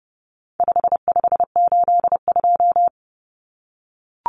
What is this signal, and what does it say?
Morse code, heard as a keyed tone, about 30 wpm, 700 Hz. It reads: 5582
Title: Numbers - 4-Digit 30wpm